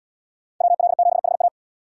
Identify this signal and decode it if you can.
Morse code, heard as a keyed tone, about 77 wpm, 700 Hz. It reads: WL7HA